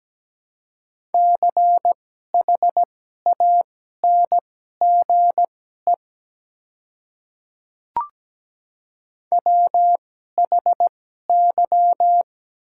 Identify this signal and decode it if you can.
Morse code, heard as a keyed tone, about 17 wpm, spaced as for 12 wpm, 700 Hz. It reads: CHANGE WHY
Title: 2x Character Spacing - Top 200 Words - 17wpm